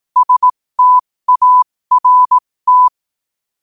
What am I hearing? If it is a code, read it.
Morse code, heard as a keyed tone, about 18 wpm, spaced as for 15 wpm, 1000 Hz. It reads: START
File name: qotdfiles / 15wpm